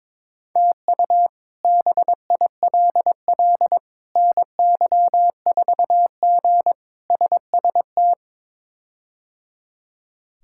Morse code, heard as a keyed tone, about 21 wpm, 700 Hz. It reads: TU BILL NY4G SST
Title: K1USN SST QSO Practice - 22wpm